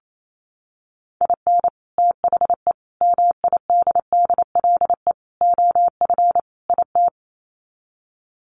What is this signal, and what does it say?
Morse code, heard as a keyed tone, about 28 wpm, 700 Hz. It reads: IN THE MIDDLE OF IT
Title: Sets of 5 Words Encore 28wpm